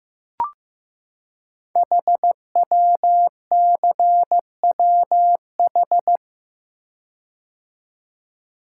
Morse code, heard as a keyed tone, about 15 wpm, 700 Hz. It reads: HWCWH
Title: Code Groups - 5 Letters - 15wpm